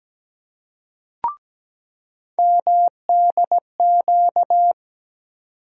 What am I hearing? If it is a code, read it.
Morse code, heard as a keyed tone, about 17 wpm, 700 Hz. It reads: MDQ
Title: USA Call Sign Postfixes - 3 Letters 17wpm